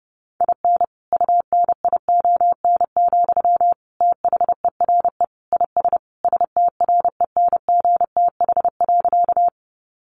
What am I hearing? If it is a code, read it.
Morse code, heard as a keyed tone, about 30 wpm, 700 Hz. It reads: IN UNION, THERE IS STRENGTH.
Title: Sentences from Top 1400 Words 30wpm